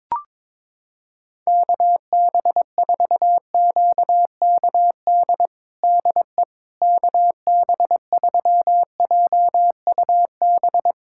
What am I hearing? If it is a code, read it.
Morse code, heard as a keyed tone, about 22 wpm, 700 Hz. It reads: KB4QKD DE KB3JUB
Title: USA Callsign DE Callsign - 6 Characters - 22wpm